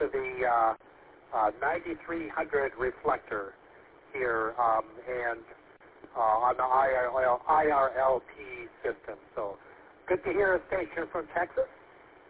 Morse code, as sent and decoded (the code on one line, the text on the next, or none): none